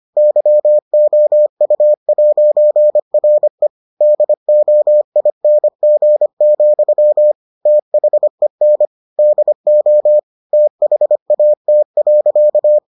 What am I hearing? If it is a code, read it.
Morse code, heard as a keyed tone, about 25 wpm, 600 Hz. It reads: YOU'RE DOING, THEN DO THAT.